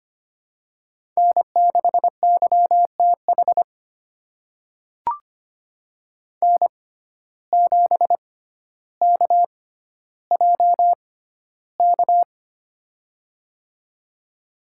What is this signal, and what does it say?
Morse code, heard as a keyed tone, about 25 wpm, spaced as for 13 wpm, 700 Hz. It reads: N6YTH N7KJK
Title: USA Call Signs - 5 Characters - 6x Character Spacing - 25wpm